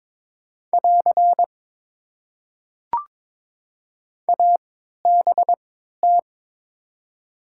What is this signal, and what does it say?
Morse code, heard as a keyed tone, about 22 wpm, spaced as for 13 wpm, 700 Hz. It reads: <AR> ABT
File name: Top 90 QSO Elements - 3x Character Spacing - 22wpm